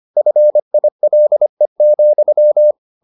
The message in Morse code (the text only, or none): FILE,